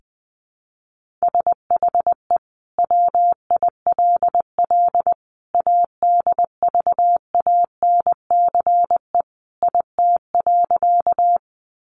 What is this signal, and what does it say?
Morse code, heard as a keyed tone, about 20 wpm, 700 Hz. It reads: SHE WILL ADVANCE IT.
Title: Sentences from Top 1400 Words 20wpm